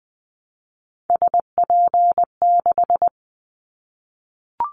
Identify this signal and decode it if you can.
Morse code, heard as a keyed tone, about 20 wpm, 700 Hz. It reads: SP6
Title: Callsign Prefixes for Top 25 DXCC - 20wpm